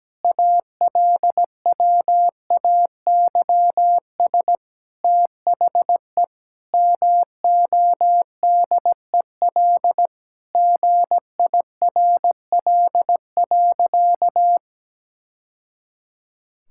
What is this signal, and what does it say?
Morse code, heard as a keyed tone, about 17 wpm, 700 Hz. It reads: ALWAYS THE MODEL GIRL.
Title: Sentences from Top 1000 Words - Review Part 5 - 17wpm